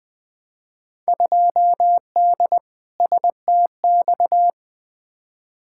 Morse code, heard as a keyed tone, about 20 wpm, 700 Hz. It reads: 2D STX
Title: Field Day Exchanges - 20wpm